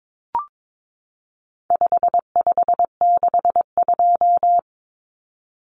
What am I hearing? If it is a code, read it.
Morse code, heard as a keyed tone, about 22 wpm, 700 Hz. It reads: 5562
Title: Common Cut Numbers - 4-Digit 22wpm